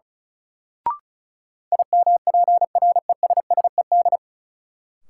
Morse code, heard as a keyed tone, about 34 wpm, 700 Hz. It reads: IMPRESSED
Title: Sentences from Top 1600 Words - New Word Drill - 35wpm